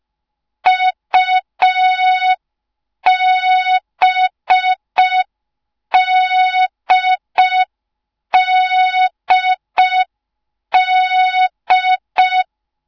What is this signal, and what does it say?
Morse code, heard as a keyed tone, about 5 wpm, 3700 Hz. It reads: UBDDD
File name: Morze-HA2MN / hang / 06BDU HA2MN